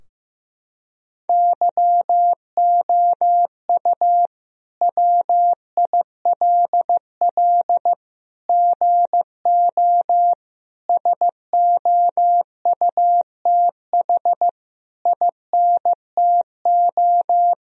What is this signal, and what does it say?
Morse code, heard as a keyed tone, about 15 wpm, 700 Hz. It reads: YOU WILL GO SOUTH INTO
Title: Sentences from Top 1000 Words - Review Part 2 - 15wpm